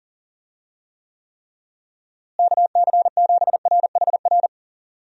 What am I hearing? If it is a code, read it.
Morse code, heard as a keyed tone, about 40 wpm, 700 Hz. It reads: KC7RHR
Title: USA Call Signs - 6 Characters - Repeated 3x - 40wpm